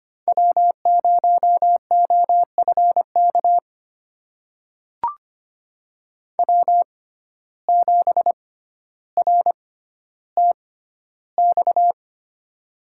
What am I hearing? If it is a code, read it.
Morse code, heard as a keyed tone, about 25 wpm, spaced as for 13 wpm, 700 Hz. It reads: W0OFK W7RTX
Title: USA Call Signs - 5 Characters - 6x Character Spacing - 25wpm